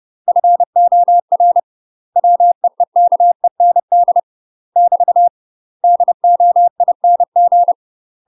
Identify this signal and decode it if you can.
Morse code, heard as a keyed tone, about 30 wpm, 700 Hz. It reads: FOR WEEKEND <BT> DOING
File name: UK MASTER QSOs TRAINING / 30WPM / UK MASTER QSOs 0051